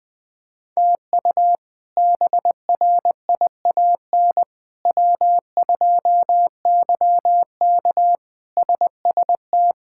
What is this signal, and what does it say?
Morse code, heard as a keyed tone, about 20 wpm, 700 Hz. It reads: TU BRIAN W2YK SST
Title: K1USN SST QSO Practice - 20wpm